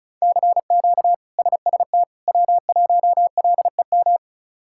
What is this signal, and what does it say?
Morse code, heard as a keyed tone, about 34 wpm, 700 Hz. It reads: CQ SST W1LEM
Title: K1USN SST QSO Practice - 35wpm